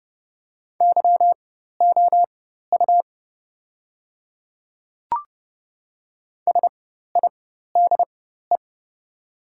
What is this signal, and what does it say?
Morse code, heard as a keyed tone, about 30 wpm, spaced as for 14 wpm, 700 Hz. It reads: YOU SIDE
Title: 4x Character Spacing - Top 100 Words - 30wpm